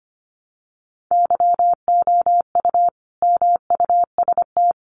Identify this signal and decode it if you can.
Morse code, heard as a keyed tone, about 25 wpm, 700 Hz. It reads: YOU MUST
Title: Sentences from Top 1000 Words 25wpm